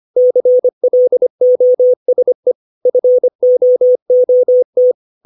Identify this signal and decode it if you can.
Morse code, heard as a keyed tone, about 25 wpm, 500 Hz. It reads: CLOSE FOOT
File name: 632 words / 632 Words 25WPM